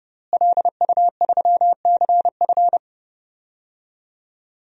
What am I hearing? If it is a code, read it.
Morse code, heard as a keyed tone, about 32 wpm, 700 Hz. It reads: LU3CF
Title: DXCC-100 Call Signs - 5 Characters - Repeated 3x - 30wpm